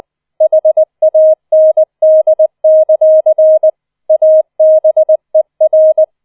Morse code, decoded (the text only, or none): HAND; ABER